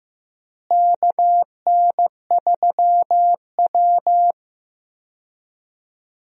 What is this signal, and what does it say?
Morse code, heard as a keyed tone, about 15 wpm, 700 Hz. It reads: KN3W
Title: USA Call Signs - 4 Characters - Repeated 2x - 15wpm